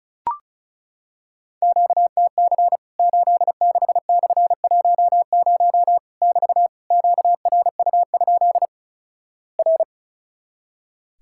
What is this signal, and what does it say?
Morse code, heard as a keyed tone, about 36 wpm, 700 Hz. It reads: QTC 86/10 <BT> QRU? R
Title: Worked All Europe DX Contest - QTC Exchange - 35wpm